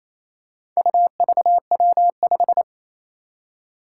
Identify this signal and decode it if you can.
Morse code, heard as a keyed tone, about 27 wpm, 700 Hz. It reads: UVW5
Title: Four Challenging Letter-Number Combinations 28wpm